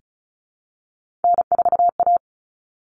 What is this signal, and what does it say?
Morse code, heard as a keyed tone, about 34 wpm, 700 Hz. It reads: N4A
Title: All Cut Numbers - 3-Digit 35wpm